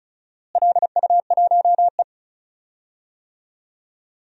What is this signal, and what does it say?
Morse code, heard as a keyed tone, about 34 wpm, 700 Hz. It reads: LU1E
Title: DXCC-100 Call Signs - 4 Characters - Repeated 3x - 35wpm